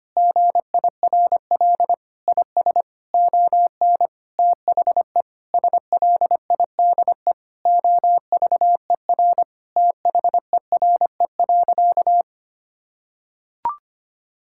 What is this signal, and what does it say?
Morse code, heard as a keyed tone, about 25 wpm, 700 Hz. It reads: GIRL IS ON THE SLIDE OVER THERE.
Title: Sentences from Top 1600 Words 25wpm